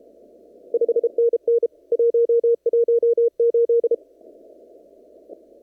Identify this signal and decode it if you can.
Morse code, heard as a keyed tone, about 29 wpm, 470 Hz. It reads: 5NN 118
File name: CW / kt5j wpx cw-op-k5pi